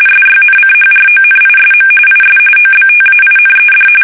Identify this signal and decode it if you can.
RTTY, about 45.45 baud. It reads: SYNCHRONISM IS OBTAINED
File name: fsk850